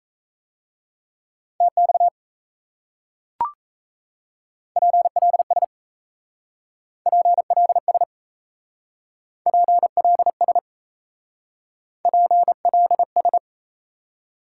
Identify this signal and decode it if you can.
Morse code, heard as a keyed tone, about 35 wpm, 700 Hz. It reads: TX PLS PLS PLS PLS